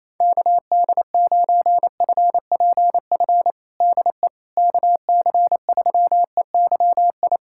Morse code, heard as a keyed tone, about 27 wpm, 700 Hz. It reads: KD9FPF DE KC3EYI
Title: USA Callsign DE Callsign - 6 Characters - 28wpm